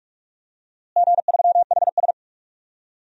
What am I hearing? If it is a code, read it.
Morse code, heard as a keyed tone, about 46 wpm, 700 Hz. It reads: G3HS
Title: DXCC-100 Call Signs - 4 Characters - Repeated 3x - 45wpm